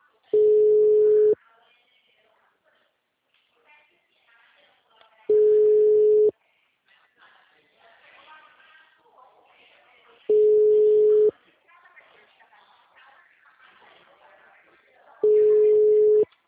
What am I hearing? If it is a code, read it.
Ringback tone (Europe, 425 Hz), with speech.